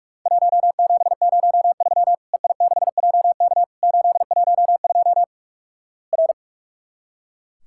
Morse code, heard as a keyed tone, about 44 wpm, 700 Hz. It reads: 1703 EI6JK 812 R
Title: Worked All Europe DX Contest - QTC Exchange - 45wpm